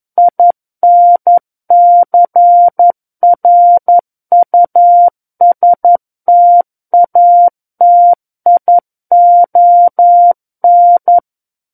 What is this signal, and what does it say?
Morse code, heard as a keyed tone, about 11 wpm, 700 Hz. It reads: INCRUSTATION
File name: 11 WPM